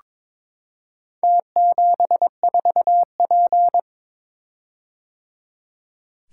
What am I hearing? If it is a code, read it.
Morse code, heard as a keyed tone, about 22 wpm, 700 Hz. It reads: T74P
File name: practice / 22wpm / Four Letter-Number Combinations - 22wpm